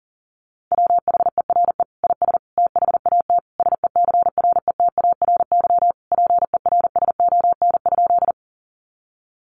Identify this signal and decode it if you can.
Morse code, heard as a keyed tone, about 40 wpm, 700 Hz. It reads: WHERE IS THAT SECRETARY PERSON?
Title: Sentences from Top 1300 Words 40wpm